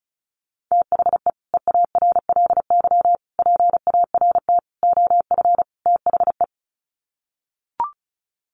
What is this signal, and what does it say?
Morse code, heard as a keyed tone, about 34 wpm, 700 Hz. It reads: THE EARLY PART OF THE
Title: Sets of 5 Words Encore2 35wpm